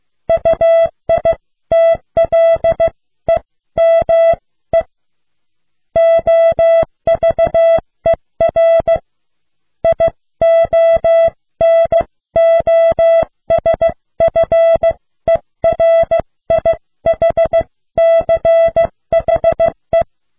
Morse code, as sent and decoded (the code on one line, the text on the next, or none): ..- .. - .-.. -- . / --- ...- . .-. / .. --- -. --- ... ..-. .-. .. .... -.-. .... .
UITLEME OVER IONOSFERIHCHE